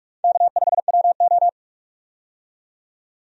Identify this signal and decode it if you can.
Morse code, heard as a keyed tone, about 47 wpm, 700 Hz. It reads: K5WO
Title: USA Call Signs - 4 Characters - 45wpm